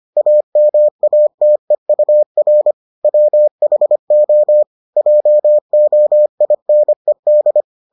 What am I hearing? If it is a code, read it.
Morse code, heard as a keyed tone, about 25 wpm, 600 Hz. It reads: AMATEUR WHO JOINED